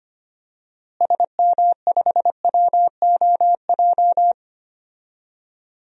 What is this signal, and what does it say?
Morse code, heard as a keyed tone, about 25 wpm, 700 Hz. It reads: SM5WOJ
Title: DXCC-100 Call Signs - 6 Characters - Repeated 3x - 25wpm